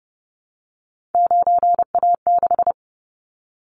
Morse code, heard as a keyed tone, about 32 wpm, 700 Hz. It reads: 9A6
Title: Callsign Prefixes for Top 100 DXCC - 30wpm